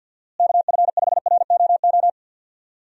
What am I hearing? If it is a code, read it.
Morse code, heard as a keyed tone, about 52 wpm, 700 Hz. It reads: KF5ROO
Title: USA Call Signs - 6 Characters - Repeated 3x - 50wpm